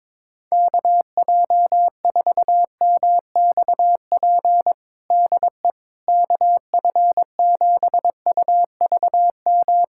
Morse code, heard as a keyed tone, about 22 wpm, 700 Hz. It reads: KJ4MXP DE KF7UVM
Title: USA Callsign DE Callsign 22wpm